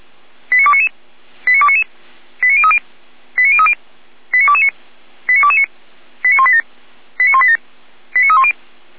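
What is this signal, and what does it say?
Five-tone selective call (ZVEI), heard as quick runs of five tones, tones of about 70 ms. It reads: 89299 89299 89030 89030 89229 89229 89187 89187 89310
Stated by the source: ZVEI five-tone selective call received by SDR, logged on the Feuerwehr Orlamünde fire brigade server.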